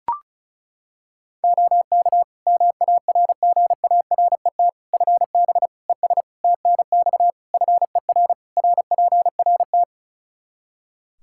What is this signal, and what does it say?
Morse code, heard as a keyed tone, about 36 wpm, 700 Hz. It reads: OK MARGARET FB ES TNX FER RPRT
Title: Practice QSO - Exchange 2 - Info - 35wpm